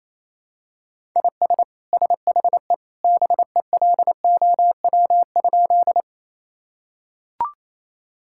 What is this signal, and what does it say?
Morse code, heard as a keyed tone, about 28 wpm, 700 Hz. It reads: IS SHE BELOW?
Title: Sentences from Top 1000 Words - Review Part 5 - 28wpm